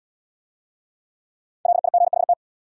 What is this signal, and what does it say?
Morse code, heard as a keyed tone, about 73 wpm, 700 Hz. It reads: LIGHT